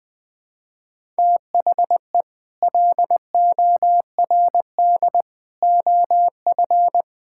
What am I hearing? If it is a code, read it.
Morse code, heard as a keyed tone, about 20 wpm, 700 Hz. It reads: THE LORD OF
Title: Sets of 9 Words - 20wpm